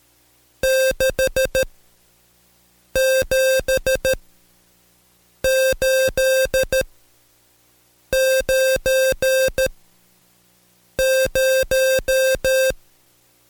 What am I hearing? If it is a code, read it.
Morse code, heard as a keyed tone, about 13 wpm, spaced as for 5 wpm, 530 Hz. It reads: 67890
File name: Code Quick / AudioDownld2 / Seg107-Day13-DayDrill 13